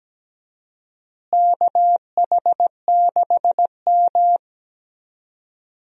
Morse code, heard as a keyed tone, about 17 wpm, 700 Hz. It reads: KH6M
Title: USA Call Signs - 4 Characters - Repeated 2x - 17wpm